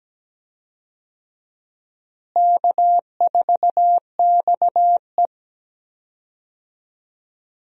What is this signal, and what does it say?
Morse code, heard as a keyed tone, about 17 wpm, 700 Hz. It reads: K4XE
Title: USA Call Signs - 4 Characters - Repeated 2x - 17wpm